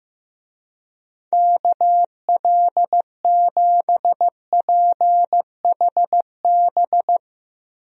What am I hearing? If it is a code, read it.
Morse code, heard as a keyed tone, about 15 wpm, 690 Hz. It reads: KL7PHB
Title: DXCC-100 Call Signs - 6 Characters - 15wpm